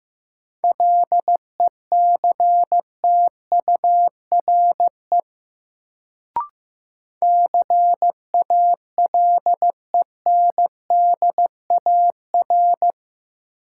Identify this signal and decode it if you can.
Morse code, heard as a keyed tone, about 15 wpm, 700 Hz. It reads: LECTURE CALENDAR
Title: Sentences from Top 1600 Words - New Word Drill Alt - 15wpm